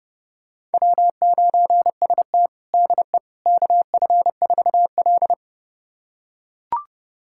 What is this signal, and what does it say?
Morse code, heard as a keyed tone, about 30 wpm, 700 Hz. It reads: W9ST DE KF4L
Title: USA Callsign DE Callsign - 4 Characters - 30wpm